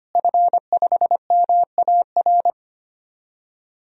Morse code, heard as a keyed tone, about 25 wpm, 700 Hz. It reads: F5MAR
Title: DXCC-100 Call Signs - 5 Characters - Repeated 2x - 25wpm